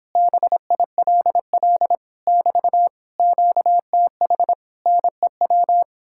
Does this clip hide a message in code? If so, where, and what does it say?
Yes: Morse code from 0.2 to 5.8 s, heard as a keyed tone, about 29 wpm, 700 Hz. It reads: BILL <BT> QTH NEW